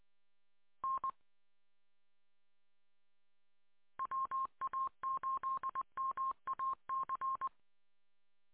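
Morse code, heard as a keyed tone, about 20 wpm, 1060 Hz. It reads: N WA8MAC
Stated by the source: Morse signal recorded on 147.200.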